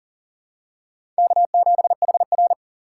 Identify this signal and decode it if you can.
Morse code, heard as a keyed tone, about 40 wpm, 700 Hz. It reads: K7HR
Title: USA Call Signs - 4 Characters - 3x Character Spacing - 40wpm